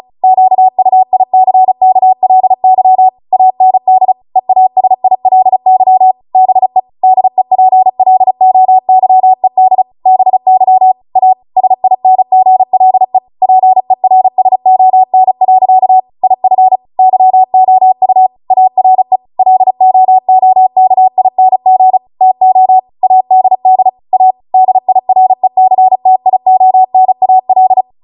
Morse code, heard as a keyed tone, about 36 wpm, 750 Hz. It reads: QUICKLY AND EASILY BE DEPLOYED BY A SINGLE PERSON. IF YOU ARE LOOKING TO ADD A DIRECTIONAL